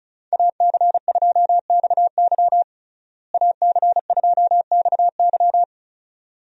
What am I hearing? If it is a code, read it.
Morse code, heard as a keyed tone, about 36 wpm, 700 Hz. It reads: AC2XY AC2XY
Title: USA Call Signs - 5 Characters - Repeated 2x - 35wpm